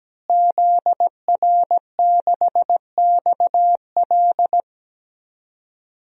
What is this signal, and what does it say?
Morse code, heard as a keyed tone, about 17 wpm, 700 Hz. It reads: ZR6XL
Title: DXCC-100 Call Signs - 5 Characters - Repeated 2x - 17wpm